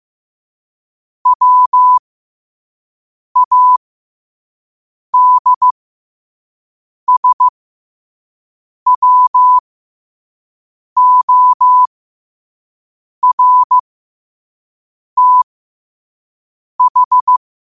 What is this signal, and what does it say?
Morse code, heard as a keyed tone, about 15 wpm, spaced as for 5 wpm, 1000 Hz. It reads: WADSWORTH
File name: qotdfiles / 5wpm